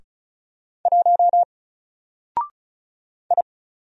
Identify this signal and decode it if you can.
Morse code, heard as a keyed tone, about 35 wpm, 700 Hz. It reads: 1 I
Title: Single Character (Letter:Number:Punctuation:Prosign) Alt 35wpm